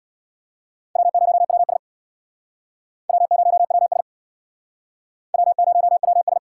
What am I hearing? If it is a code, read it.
Morse code, heard as a keyed tone, about 68 wpm, 700 Hz. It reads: W0WS W0WS W0WS